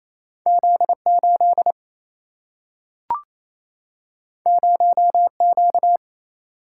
Morse code, heard as a keyed tone, about 28 wpm, 700 Hz. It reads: Z8 0Q
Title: Single-Letter Single-Number 28wpm